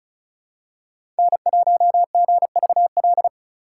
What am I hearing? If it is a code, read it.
Morse code, heard as a keyed tone, about 36 wpm, 700 Hz. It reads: N1GVL
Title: USA Call Signs - 3x Character Spacing - 35wpm